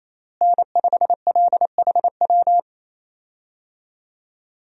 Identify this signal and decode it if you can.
Morse code, heard as a keyed tone, about 28 wpm, 700 Hz. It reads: N5LHW
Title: USA Call Signs - 5 Characters - Repeated 3x - 28wpm